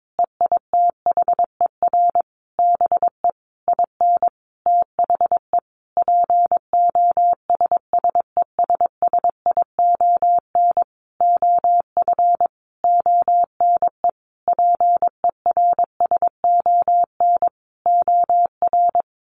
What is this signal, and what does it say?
Morse code, heard as a keyed tone, about 22 wpm, 700 Hz. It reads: EITHER BE IN THE POSSESSION OF ONE PERSON OR
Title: Sentences from Top 1400 Words 22wpm